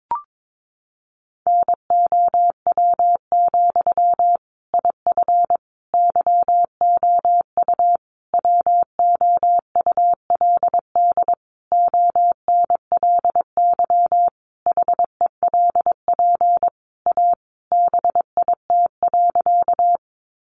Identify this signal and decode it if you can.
Morse code, heard as a keyed tone, about 22 wpm, 700 Hz. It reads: NOW, IF YOU WOULD ONLY HELP A BIT.
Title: Sentences from Top 900 Words 22wpm